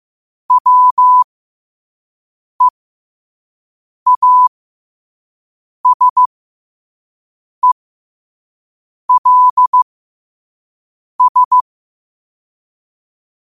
Morse code, heard as a keyed tone, about 15 wpm, spaced as for 5 wpm, 1000 Hz. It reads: WEASELS
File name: qotdfiles / 5wpm